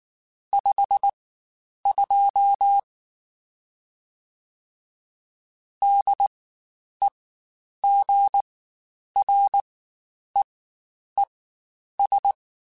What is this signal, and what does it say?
Morse code, heard as a keyed tone, about 19 wpm, spaced as for 9 wpm, 800 Hz. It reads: 52 DEGREES
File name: Spaced Characters / Spaced Characters-mp3s / Weather 4 10 sp @18